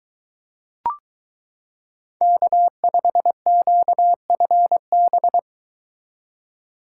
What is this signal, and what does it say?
Morse code, heard as a keyed tone, about 23 wpm, 700 Hz. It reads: K5QFB